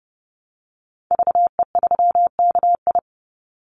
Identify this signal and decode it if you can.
Morse code, heard as a keyed tone, about 30 wpm, 700 Hz. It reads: VE3KI
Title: Call Signs - 30wpm